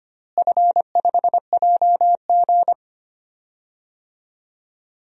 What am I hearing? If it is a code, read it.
Morse code, heard as a keyed tone, about 25 wpm, 700 Hz. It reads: F5JG
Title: DXCC-100 Call Signs - 4 Characters - Repeated 2x - 25wpm